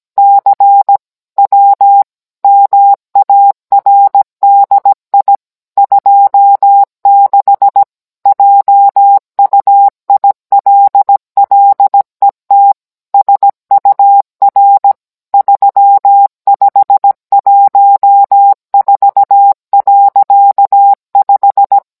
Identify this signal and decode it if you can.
Morse code, heard as a keyed tone, about 17 wpm, 800 Hz. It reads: CW MARDI 26 JUILLET SUR 3514.5